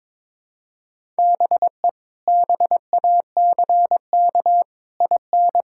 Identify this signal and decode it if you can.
Morse code, heard as a keyed tone, about 22 wpm, 700 Hz. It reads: BE BACK IN
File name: practice / 22wpm / Build A Phrase - 5-Word Phrases - 22wpm